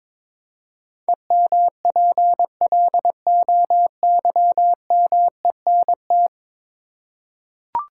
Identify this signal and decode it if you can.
Morse code, heard as a keyed tone, about 22 wpm, 700 Hz. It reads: EMPLOYMENT
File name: practice / 22wpm / 10 Letter Words - First 3 Letters Given - 22wpm